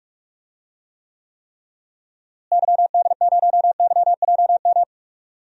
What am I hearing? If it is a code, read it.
Morse code, heard as a keyed tone, about 41 wpm, 700 Hz. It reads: YD0YJM
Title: DXCC-100 Call Signs - 6 Characters - Repeated 3x - 45wpm